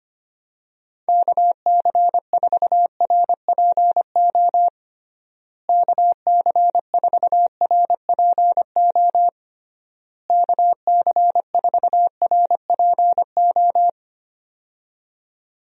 Morse code, heard as a keyed tone, about 25 wpm, 700 Hz. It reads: KC4RPO KC4RPO KC4RPO
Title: USA Call Signs - 6 Characters - Repeated 3x - 25wpm